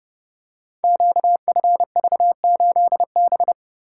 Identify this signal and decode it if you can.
Morse code, heard as a keyed tone, about 30 wpm, 700 Hz. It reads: QFV8B
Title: Code Groups - 5 Letter:Numbers - 30wpm